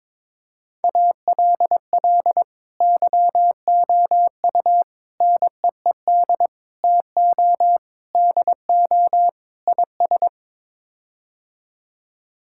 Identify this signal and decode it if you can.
Morse code, heard as a keyed tone, about 22 wpm, 700 Hz. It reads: ALL YOU NEED TO DO IS
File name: practice / 22wpm / Sets of 6 Words - 22wpm